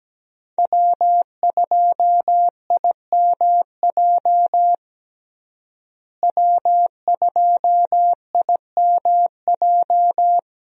Morse code, heard as a keyed tone, about 17 wpm, 700 Hz. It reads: W2IMJ W2IMJ
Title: USA Call Signs - 5 Characters - Repeated 3x - 17wpm